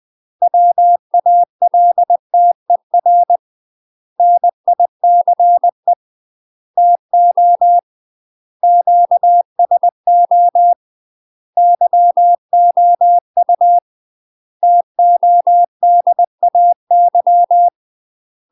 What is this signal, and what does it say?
Morse code, heard as a keyed tone, about 20 wpm, 700 Hz. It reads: WALTER NICE TO QSO YOU TODAY
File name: UK MASTER QSOs TRAINING / 20WPM / UK MASTER QSOs 0015